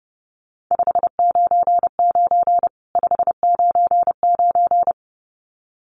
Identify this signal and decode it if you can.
Morse code, heard as a keyed tone, about 30 wpm, 700 Hz. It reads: 599 599
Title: Common QSO Phrases 30wpm